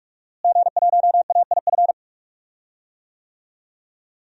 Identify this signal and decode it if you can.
Morse code, heard as a keyed tone, about 46 wpm, 700 Hz. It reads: G1AIF
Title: DXCC-100 Call Signs - 5 Characters - Repeated 3x - 45wpm